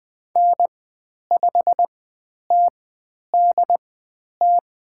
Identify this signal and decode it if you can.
Morse code, heard as a keyed tone, about 20 wpm, spaced as for 10 wpm, 700 Hz. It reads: N5TDT
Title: 095 Lesson 16g - Letter c - Call Signs 20wpm